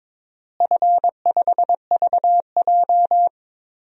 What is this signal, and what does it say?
Morse code, heard as a keyed tone, about 22 wpm, 700 Hz. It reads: F5VJ
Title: DXCC-100 Call Signs - 4 Characters - Repeated 2x - 22wpm